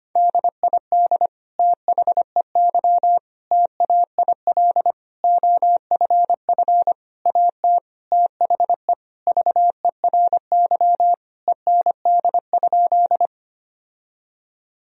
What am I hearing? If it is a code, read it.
Morse code, heard as a keyed tone, about 25 wpm, 700 Hz. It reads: DID THEY TAIL OFF AT THE VERY END?
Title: Sentences from Top 1000 Words - Review Part 3 - 25wpm